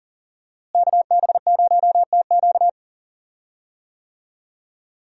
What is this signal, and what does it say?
Morse code, heard as a keyed tone, about 40 wpm, 700 Hz. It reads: KB0TQ
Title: USA Call Signs - 5 Characters - Repeated 3x - 40wpm